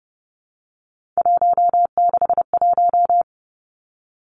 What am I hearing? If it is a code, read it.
Morse code, heard as a keyed tone, about 30 wpm, 700 Hz. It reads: 161
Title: Three Challenging Letter-Number Combinations 30wpm